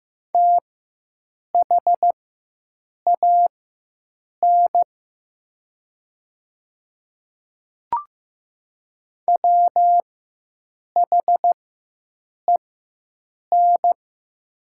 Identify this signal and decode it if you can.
Morse code, heard as a keyed tone, about 15 wpm, spaced as for 7 wpm, 700 Hz. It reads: THAN WHEN
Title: 4x Character Spacing - Top 100 Words - 15wpm